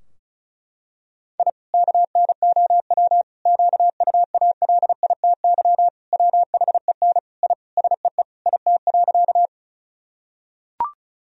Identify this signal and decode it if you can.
Morse code, heard as a keyed tone, about 34 wpm, 700 Hz. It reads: I KNOW QUALITY WHEN I SEE IT.
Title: Sentences from Top 1000 Words - Review Part 4 - 35wpm